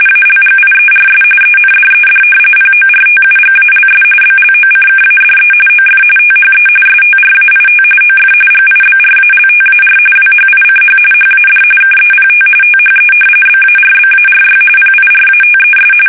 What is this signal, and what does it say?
RTTY, about 45.45 baud. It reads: ALGIERS ENDED PORTUGALS BLOCKADE OF THE STRAIT OF GIBRALTAR WHICH HAD KEPT THE BARBARY PIRATES
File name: fsk850 / usnavy1